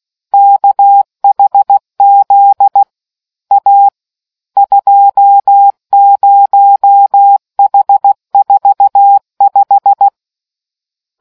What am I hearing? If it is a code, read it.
Morse code, heard as a keyed tone, about 16 wpm, 800 Hz. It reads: KHZ A 20H45